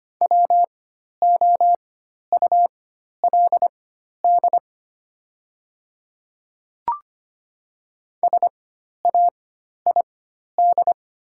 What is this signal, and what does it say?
Morse code, heard as a keyed tone, about 25 wpm, spaced as for 12 wpm, 700 Hz. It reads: WOULD SAID
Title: 4x Character Spacing - Top 100 Words - 25wpm